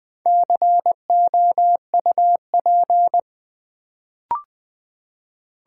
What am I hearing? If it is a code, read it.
Morse code, heard as a keyed tone, about 20 wpm, 700 Hz. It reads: COUP
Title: Build A Word - 1 Letter At A Time - 4-Letter Words - 20wpm